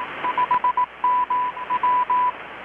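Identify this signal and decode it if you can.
Morse code, heard as a keyed tone, about 18 wpm, 990 Hz. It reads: 5MW
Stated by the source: Morse from KPH's Night of Nights XX maritime radiotelegraphy event, received in Chiapas, Mexico about 3,800 km away on a Kenwood R-600 with a grounded 10 m long-wire antenna.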